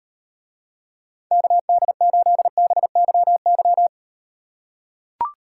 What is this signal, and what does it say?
Morse code, heard as a keyed tone, about 38 wpm, 700 Hz. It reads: KD8BYY